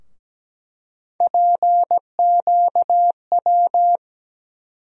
Morse code, heard as a keyed tone, about 17 wpm, 700 Hz. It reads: PQW
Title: USA Call Sign Postfixes - 3 Letters 17wpm